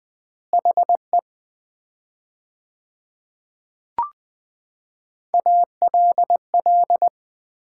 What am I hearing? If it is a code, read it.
Morse code, heard as a keyed tone, about 20 wpm, 700 Hz. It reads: HE ALL
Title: 4x Word-Spacing - Sets of 4 Words - 20wpm